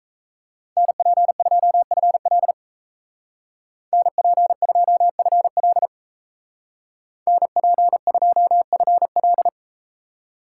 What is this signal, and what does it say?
Morse code, heard as a keyed tone, about 38 wpm, 700 Hz. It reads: NP2FL NP2FL NP2FL